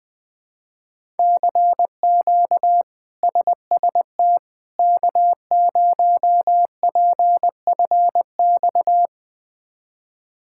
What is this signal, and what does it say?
Morse code, heard as a keyed tone, about 21 wpm, 700 Hz. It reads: CQ SST K0PFX
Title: K1USN SST QSO Practice - 20wpm